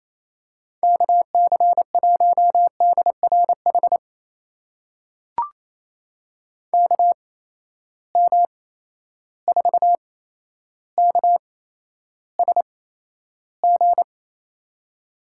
Morse code, heard as a keyed tone, about 28 wpm, spaced as for 12 wpm, 700 Hz. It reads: KC1DRH KM4KSG
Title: USA Call Signs - 6 Characters - 8x Character Spacing - 28wpm